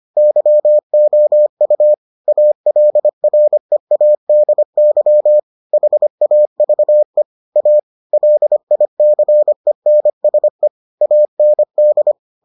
Morse code, heard as a keyed tone, about 25 wpm, 600 Hz. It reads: YOU ALREADY HAVE A LICENSE AND